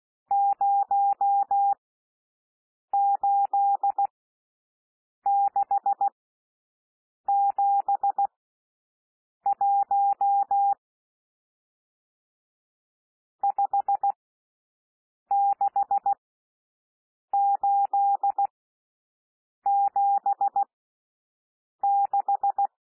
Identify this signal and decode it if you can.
Morse code, heard as a keyed tone, about 16 wpm, spaced as for 6 wpm, 800 Hz. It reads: 08671 56876
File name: SOUND NUMBERS 16 wpm